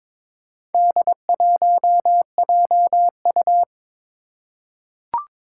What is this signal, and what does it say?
Morse code, heard as a keyed tone, about 22 wpm, 700 Hz. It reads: D1JU
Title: Four Challenging Letter-Number Combinations 22wpm